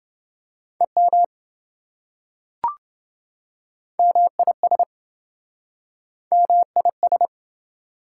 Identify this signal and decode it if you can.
Morse code, heard as a keyed tone, about 27 wpm, 700 Hz. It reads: EM MIS MIS